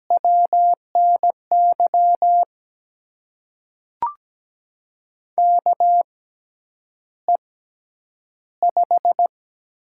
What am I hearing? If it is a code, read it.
Morse code, heard as a keyed tone, about 17 wpm, spaced as for 9 wpm, 700 Hz. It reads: WNY KE5
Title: USA Call Signs - 6 Characters - 6x Character Spacing - 17wpm